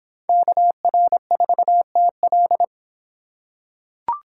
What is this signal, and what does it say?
Morse code, heard as a keyed tone, about 24 wpm, 700 Hz. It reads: KR4TL